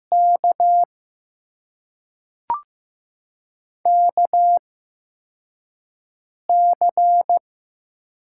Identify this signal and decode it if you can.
Morse code, heard as a keyed tone, about 15 wpm, spaced as for 4 wpm, 700 Hz. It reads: K KC
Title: USA Call Signs - 6 Characters - 8x Character Spacing - 15wpm